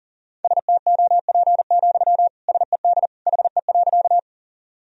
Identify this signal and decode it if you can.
Morse code, heard as a keyed tone, about 37 wpm, 700 Hz. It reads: STOP, SED HE.
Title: Sentences from Top 300 Words and Abbreviations 40wpm